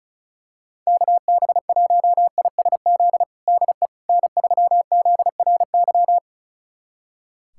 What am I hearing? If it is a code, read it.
Morse code, heard as a keyed tone, about 36 wpm, 700 Hz. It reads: KB1ISZ DE N3ZRY
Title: USA Callsign DE Callsign 35wpm